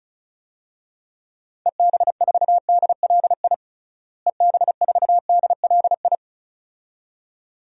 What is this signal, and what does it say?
Morse code, heard as a keyed tone, about 36 wpm, 700 Hz. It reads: EB4DLI EB4DLI
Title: DXCC-100 Call Signs - 6 Characters - Repeated 2x - 35wpm